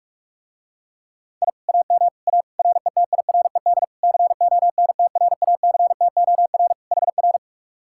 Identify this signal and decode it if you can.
Morse code, heard as a keyed tone, about 46 wpm, 700 Hz. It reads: I AM A RETIRED CONTRACTOR HR